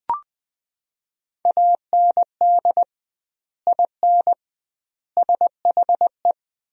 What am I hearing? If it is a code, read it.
Morse code, heard as a keyed tone, about 20 wpm, 700 Hz. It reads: AND IN SHE
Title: 2x Word-Spacing - Sentences from Top 200 Words - 20wpm